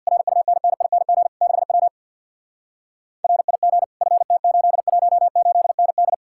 Morse code, heard as a keyed tone, about 63 wpm, 700 Hz. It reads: RUNNING 6W RIG FT818ND